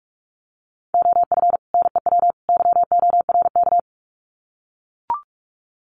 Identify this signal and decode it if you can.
Morse code, heard as a keyed tone, about 44 wpm, 700 Hz. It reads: OF NEW YORK
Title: Sets of 3 Words 45wpm